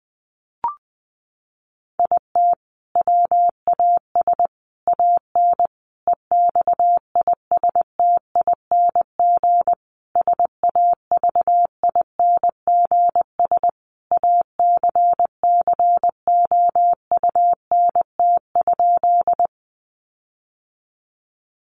Morse code, heard as a keyed tone, about 20 wpm, 700 Hz. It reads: IT WAS AN EXISTING SAVINGS ACCOUNT?
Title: Sentences from Top 1500 Words 20wpm